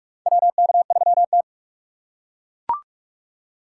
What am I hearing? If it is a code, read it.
Morse code, heard as a keyed tone, about 43 wpm, 700 Hz. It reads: WK3T
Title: USA Call Signs - 4 Characters - 2x Character Spacing - 45wpm